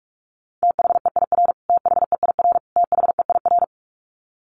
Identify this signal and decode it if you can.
Morse code, heard as a keyed tone, about 44 wpm, 700 Hz. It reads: THEIR THEIR THEIR
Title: Top 200 Words Repeated 45wpm